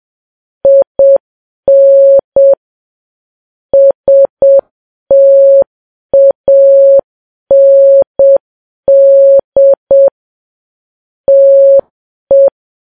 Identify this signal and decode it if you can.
Morse code, heard as a keyed tone, about 7 wpm, 550 Hz. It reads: IN STAND TE
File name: Files Web tls / Morse on9fv / morse7kps